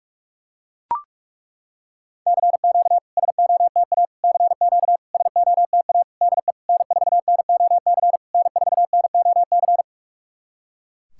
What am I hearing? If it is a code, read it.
Morse code, heard as a keyed tone, about 41 wpm, 690 Hz. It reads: CQ SOTA CQ SOTA DE N4NOC N4NOC
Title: SOTA QSO Practice 45wpm